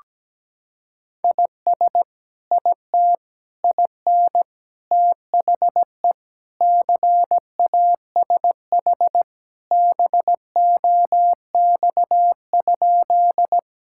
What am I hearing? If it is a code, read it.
Morse code, heard as a keyed tone, about 17 wpm, 700 Hz. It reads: IS IT IN THE CASH BOX?
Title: Sentences from Top 1000 Words - Review Part 5 - 17wpm